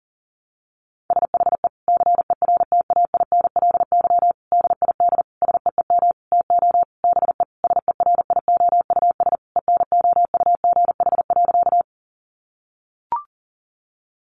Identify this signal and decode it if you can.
Morse code, heard as a keyed tone, about 40 wpm, 700 Hz. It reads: SHE CERTAINLY DID SEEM TO BE SERIOUS ENOUGH.
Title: Sentences from Top 1100 Words 40wpm